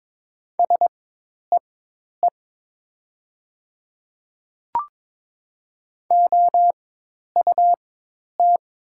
Morse code, heard as a keyed tone, about 22 wpm, spaced as for 10 wpm, 700 Hz. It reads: SEE OUT
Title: 4x Character Spacing - Top 200 Words - 22wpm